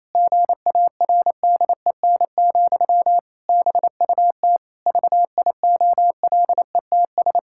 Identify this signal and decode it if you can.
Morse code, heard as a keyed tone, about 27 wpm, 700 Hz. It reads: GARDEN, BUT VIOLETS
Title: Pangram Challenge v2 - 28wpm